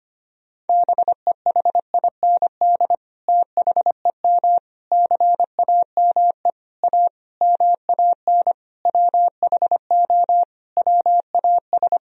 Morse code, heard as a keyed tone, about 25 wpm, 700 Hz. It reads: BEHIND THEM CAME A MAN WHO WAS
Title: Sentences from Top 1000 Words - Review Part 4 - 25wpm